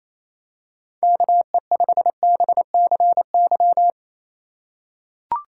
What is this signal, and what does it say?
Morse code, heard as a keyed tone, about 28 wpm, 700 Hz. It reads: KE5BCY
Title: K1USN SST QSO Practice - 28wpm